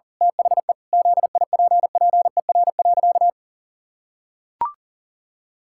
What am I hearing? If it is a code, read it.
Morse code, heard as a keyed tone, about 40 wpm, 700 Hz. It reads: THE ZIPPER.
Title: Pangram Challenge v2 - 40wpm